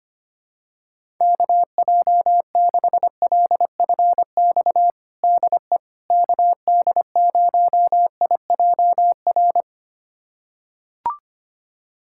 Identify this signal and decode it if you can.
Morse code, heard as a keyed tone, about 25 wpm, 700 Hz. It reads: KJ6LFX DE KD0IJR
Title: USA Callsign DE Callsign - 6 Characters - 25wpm